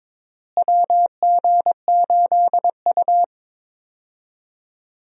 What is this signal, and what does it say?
Morse code, heard as a keyed tone, about 22 wpm, 700 Hz. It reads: WG8U
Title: USA Call Signs - 4 Characters - Repeated 2x - 22wpm